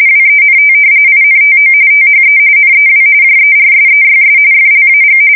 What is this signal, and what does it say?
RTTY, about 45.45 baud. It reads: WENT INTO THE NEW JERSEYS FLAG
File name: fsk170 / battleshipnj